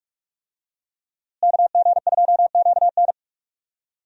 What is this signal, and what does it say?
Morse code, heard as a keyed tone, about 46 wpm, 700 Hz. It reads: KG2QN
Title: USA Call Signs - 5 Characters - Repeated 2x - 45wpm